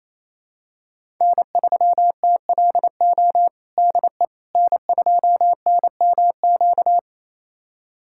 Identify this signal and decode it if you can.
Morse code, heard as a keyed tone, about 27 wpm, 700 Hz. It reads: N3TLO DE N2NMQ
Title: USA Callsign DE Callsign - 5 Characters - 28wpm